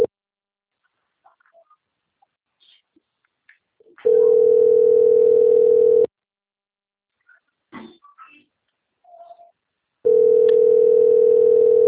Ringback tone (North America), 440 + 480 Hz.